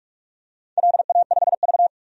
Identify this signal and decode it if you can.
Morse code, heard as a keyed tone, about 46 wpm, 700 Hz. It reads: LA5V
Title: DXCC-100 Call Signs - 4 Characters - 45wpm